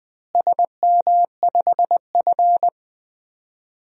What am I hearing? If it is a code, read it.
Morse code, heard as a keyed tone, about 20 wpm, 700 Hz. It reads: SM5F
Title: DXCC-100 Call Signs - 4 Characters - Repeated 3x - 20wpm